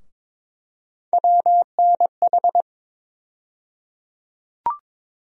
Morse code, heard as a keyed tone, about 21 wpm, 700 Hz. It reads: WNH
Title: USA Call Sign Postfixes - 3 Letters 22wpm